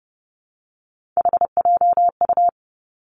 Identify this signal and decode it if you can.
Morse code, heard as a keyed tone, about 30 wpm, 700 Hz. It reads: HJU
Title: Three Challenging Letter-Number Combinations 30wpm